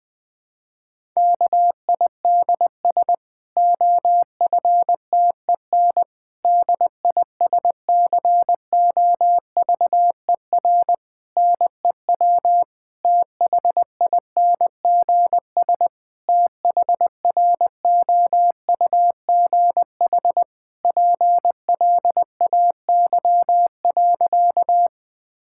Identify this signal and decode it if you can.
Morse code, heard as a keyed tone, about 20 wpm, 700 Hz. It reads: KIDS OFTEN DISCOVER NEW THINGS THROUGH PLAY.
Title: Difficult Sentences with Hints - 20wpm